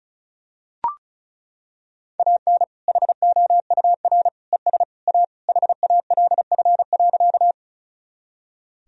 Morse code, heard as a keyed tone, about 34 wpm, 700 Hz. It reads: AN HOUR ES A HALF.
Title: Sentences from Top 400 Words and Abbreviations 35wpm